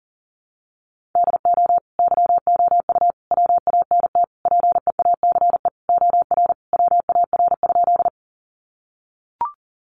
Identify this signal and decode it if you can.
Morse code, heard as a keyed tone, about 40 wpm, 700 Hz. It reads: DO YOU WANT PEACE OR WAR?
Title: Sentences from Top 1500 Words 40wpm